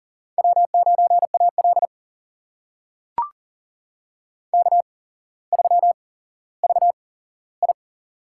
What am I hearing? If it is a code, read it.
Morse code, heard as a keyed tone, about 40 wpm, spaced as for 17 wpm, 700 Hz. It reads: W9AL K3VI
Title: USA Call Signs - 4 Characters - 8x Character Spacing - 40wpm